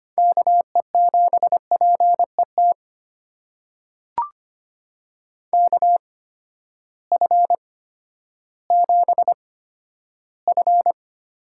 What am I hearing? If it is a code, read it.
Morse code, heard as a keyed tone, about 25 wpm, spaced as for 12 wpm, 700 Hz. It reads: KE7PET KF7F
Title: USA Call Signs - 6 Characters - 8x Character Spacing - 25wpm